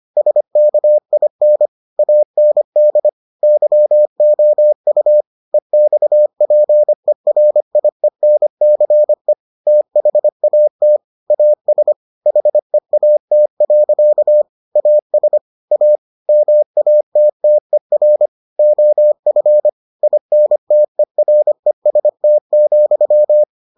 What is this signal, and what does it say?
Morse code, heard as a keyed tone, about 25 wpm, 600 Hz. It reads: SKIN AND YOU EXPERIENCE THAT AS HEAT. AS A MATTER OF INTEREST,